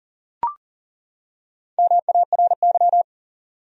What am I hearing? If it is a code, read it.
Morse code, heard as a keyed tone, about 40 wpm, 700 Hz. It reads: MARY
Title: USA Common Names - Max 5 Letters - 40wpm